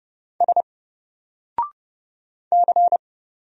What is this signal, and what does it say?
Morse code, heard as a keyed tone, about 31 wpm, 700 Hz. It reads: S C
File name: Single-Letter-Number-30wpm